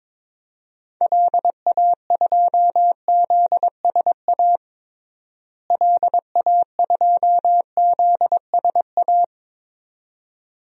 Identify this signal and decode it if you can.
Morse code, heard as a keyed tone, about 22 wpm, 700 Hz. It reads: LA2ZSA LA2ZSA
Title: DXCC-100 Call Signs - 6 Characters - Repeated 2x - 22wpm